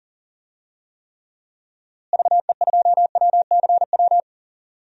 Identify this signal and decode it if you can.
Morse code, heard as a keyed tone, about 40 wpm, 700 Hz. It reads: VE2WCW
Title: DXCC-100 Call Signs - 6 Characters - Repeated 3x - 40wpm